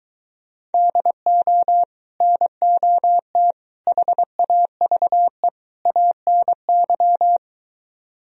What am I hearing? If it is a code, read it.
Morse code, heard as a keyed tone, about 23 wpm, 700 Hz. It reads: DO NOT HAVE ANY